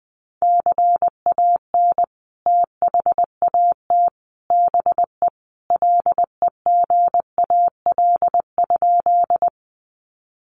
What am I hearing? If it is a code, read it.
Morse code, heard as a keyed tone, about 20 wpm, 700 Hz. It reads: CAN THAT BE LEGAL?
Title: Sentences from Top 1200 Words 20wpm